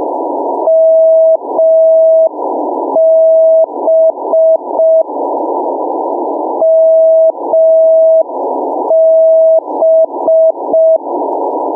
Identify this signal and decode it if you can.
Morse code, heard as a keyed tone, about 5 wpm, 650 Hz. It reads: MB MB
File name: mw / 0437 499 mb beacon